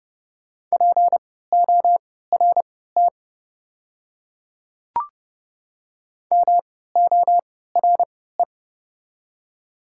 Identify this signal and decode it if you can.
Morse code, heard as a keyed tone, about 31 wpm, spaced as for 17 wpm, 700 Hz. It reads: PORT MORE
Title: 3x Character Spacing - Top 200 Words - 30wpm